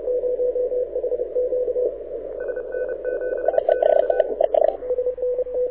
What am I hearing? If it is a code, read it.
Morse code, heard as a keyed tone, about 33 wpm, 1500 Hz, with radio noise. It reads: SN7Q
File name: audio zawody / IARU 2015 CW SN7Q